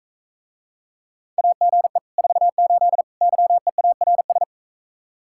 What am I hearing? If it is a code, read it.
Morse code, heard as a keyed tone, about 44 wpm, 700 Hz. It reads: AGE 48 YEARS